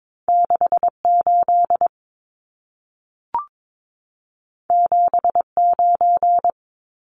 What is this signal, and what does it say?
Morse code, heard as a keyed tone, about 22 wpm, 700 Hz. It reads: 68 79
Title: Numbers - 2-Digit 22wpm